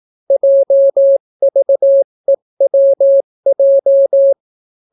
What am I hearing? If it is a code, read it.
Morse code, heard as a keyed tone, about 18 wpm, 550 Hz. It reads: JVEWJ